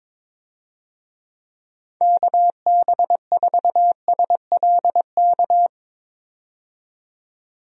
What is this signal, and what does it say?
Morse code, heard as a keyed tone, about 22 wpm, 700 Hz. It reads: KB4SLK
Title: USA Call Signs - 6 Characters - 22wpm